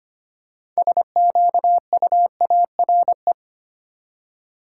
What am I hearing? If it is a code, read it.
Morse code, heard as a keyed tone, about 25 wpm, 700 Hz. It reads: SQUARE
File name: practice / 25wpm / Easy Words with Hints - 25wpm